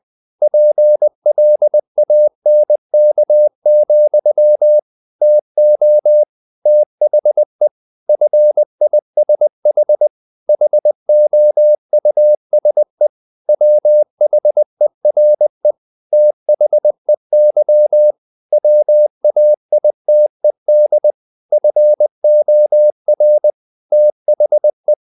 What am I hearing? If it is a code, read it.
Morse code, heard as a keyed tone, about 20 wpm, 600 Hz. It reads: PLANK, TO THE FISH HOUSE WHERE THEY WAITED FOR THE
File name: the old man and the sea / the old man and the sea part 1 20wpm